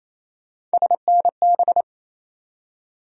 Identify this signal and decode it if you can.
Morse code, heard as a keyed tone, about 28 wpm, 700 Hz. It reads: SNB
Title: USA Call Sign Postfixes - 3 Letters 28wpm